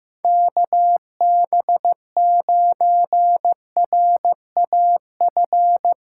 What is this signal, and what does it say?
Morse code, heard as a keyed tone, about 15 wpm, 700 Hz. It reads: KB9RAF
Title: USA Callsign DE Callsign 15wpm